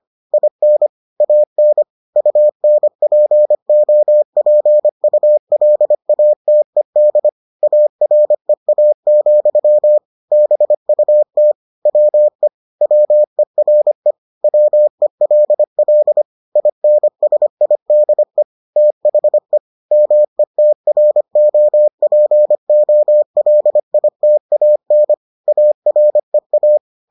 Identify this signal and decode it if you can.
Morse code, heard as a keyed tone, about 25 wpm, 600 Hz. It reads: IN AN UNPOPULATED AREA, BUT WE WERE WELL INSIDE THE METROPOLITAN AREA